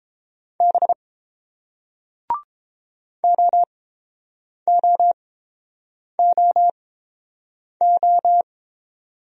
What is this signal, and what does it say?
Morse code, heard as a keyed tone, about 30 wpm, spaced as for 7 wpm, 700 Hz. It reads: B OOOO